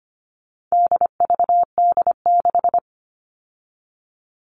Morse code, heard as a keyed tone, about 25 wpm, 700 Hz. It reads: DVD6
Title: All Cut Numbers - 4-Digit 25wpm